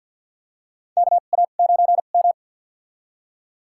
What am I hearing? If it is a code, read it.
Morse code, heard as a keyed tone, about 50 wpm, spaced as for 36 wpm, 700 Hz. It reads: KA9M
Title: USA Call Signs - 4 Characters - 2x Character Spacing - 50wpm